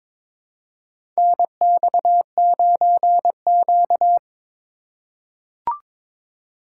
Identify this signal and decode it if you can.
Morse code, heard as a keyed tone, about 21 wpm, 700 Hz. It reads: NX9Q
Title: USA Call Signs - 4 Characters - 4x Character Spacing - 22wpm